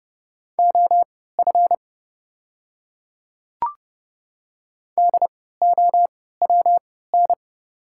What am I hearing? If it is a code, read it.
Morse code, heard as a keyed tone, about 30 wpm, spaced as for 17 wpm, 700 Hz. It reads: OF DOWN
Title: 3x Character Spacing - Top 100 Words - 30wpm